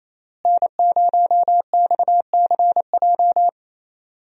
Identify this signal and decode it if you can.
Morse code, heard as a keyed tone, about 28 wpm, 700 Hz. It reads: N0XCJ
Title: USA Call Signs - 5 Characters - Repeated 3x - 28wpm